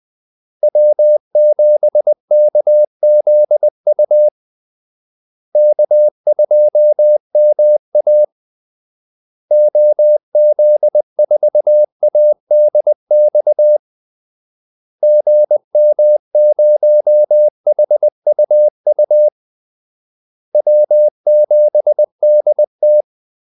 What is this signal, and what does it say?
Morse code, heard as a keyed tone, about 20 wpm, 600 Hz. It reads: W7KZU K2MA OZ4ADX GM0HUU W7DT
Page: Kaldesignaler 20 wpm